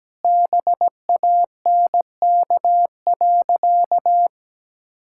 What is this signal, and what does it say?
Morse code, heard as a keyed tone, about 17 wpm, 700 Hz. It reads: BANK.
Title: Sentences from Top 1000 Words - Review Part 6 - 17wpm